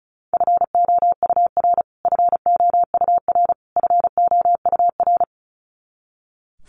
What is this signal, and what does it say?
Morse code, heard as a keyed tone, about 34 wpm, 700 Hz. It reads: FOUR FOUR FOUR
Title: Top 300 Words Repeated 35wpm